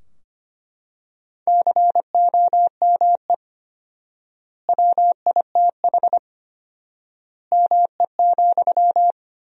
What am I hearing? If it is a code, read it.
Morse code, heard as a keyed tone, about 25 wpm, 700 Hz. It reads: COME WITH ME,
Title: 4x Word-Spacing - Sentences from Top 200 Words - 25wpm